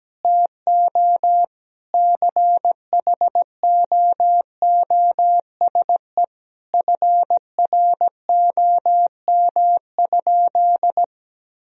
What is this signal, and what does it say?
Morse code, heard as a keyed tone, about 17 wpm, 700 Hz. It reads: TO CHOOSE FROM?
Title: Sentences from Top 1000 Words - Review Part 4 - 17wpm